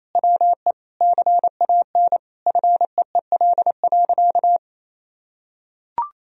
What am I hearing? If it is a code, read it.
Morse code, heard as a keyed tone, about 28 wpm, 700 Hz. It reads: WE CAN FEEL.
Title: Sentences from Top 1000 Words - Review Part 3 - 28wpm